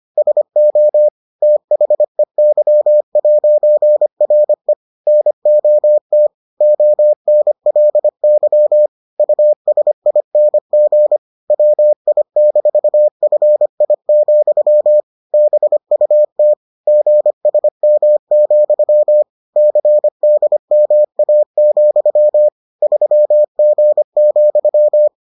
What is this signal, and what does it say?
Morse code, heard as a keyed tone, about 25 wpm, 600 Hz. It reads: SO THEY'RE NOT ONLY USING WI-FI, BUT GSM, CDMA, 3G,